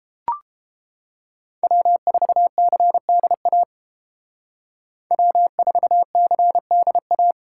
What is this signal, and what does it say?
Morse code, heard as a keyed tone, about 29 wpm, 700 Hz. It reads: W4CDA W4CDA